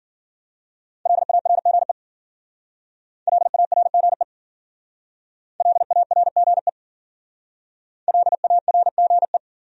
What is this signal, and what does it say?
Morse code, heard as a keyed tone, about 50 wpm, 700 Hz. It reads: LARGE LARGE LARGE LARGE
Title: Speed-Racing Top 500 Words 40wpm